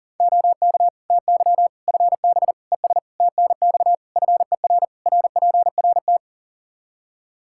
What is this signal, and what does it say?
Morse code, heard as a keyed tone, about 40 wpm, 700 Hz. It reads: OK TY FB ES TNX FER RPRT
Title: K1USN SST QSO Practice 40wpm